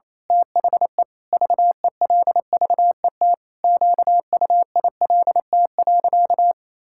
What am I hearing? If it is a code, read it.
Morse code, heard as a keyed tone, about 28 wpm, 700 Hz. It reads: THE VELVET QUILT.
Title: Pangram Challenge v2 - 28wpm